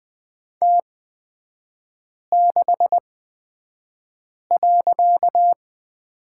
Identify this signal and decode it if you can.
Morse code, heard as a keyed tone, about 20 wpm, spaced as for 5 wpm, 690 Hz. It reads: T6.
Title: 255 Lesson 41e - Period - Three Characters 20wpm